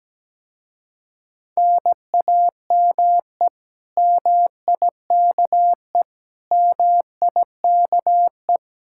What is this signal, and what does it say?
Morse code, heard as a keyed tone, about 18 wpm, 700 Hz. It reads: NAME MIKE MIKE
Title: Speed-Racing Common QSO Phrases Encore - 17wpm